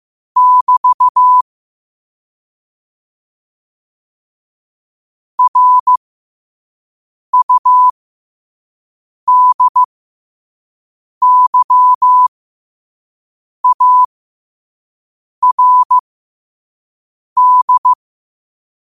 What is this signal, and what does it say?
Morse code, heard as a keyed tone, about 15 wpm, spaced as for 5 wpm, 1000 Hz. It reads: <BT> RUDYARD
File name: qotdfiles / 5wpm